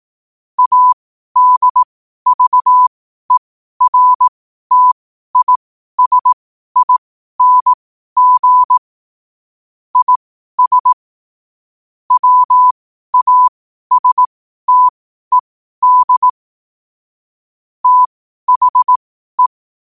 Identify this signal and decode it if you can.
Morse code, heard as a keyed tone, about 18 wpm, spaced as for 12 wpm, 1000 Hz. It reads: ADVERTISING IS WASTED THE
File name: qotdfiles / 12wpm